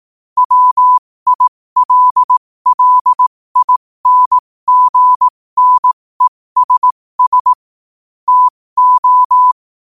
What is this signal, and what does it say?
Morse code, heard as a keyed tone, about 18 wpm, spaced as for 15 wpm, 1000 Hz. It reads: WILLINGNESS TO
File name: qotdfiles / 15wpm